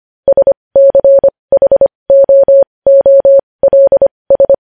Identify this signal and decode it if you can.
Morse code, heard as a keyed tone, about 25 wpm, spaced as for 20 wpm, 550 Hz. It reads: SCHOOLS
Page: news-headlines-in-morse-code-at-20-wpm